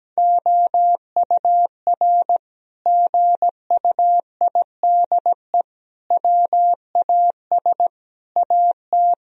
Morse code, heard as a keyed tone, about 17 wpm, 700 Hz. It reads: OUR GUIDE WAS AT
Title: Sentences from Top 1000 Words - Review Part 3 - 17wpm